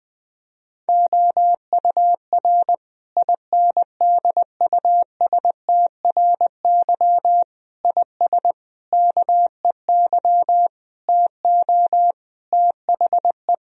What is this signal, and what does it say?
Morse code, heard as a keyed tone, about 20 wpm, 700 Hz. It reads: OUR INDUSTRY IS KEY TO THE
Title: Sentences from Top 1000 Words - Review Part 4 - 20wpm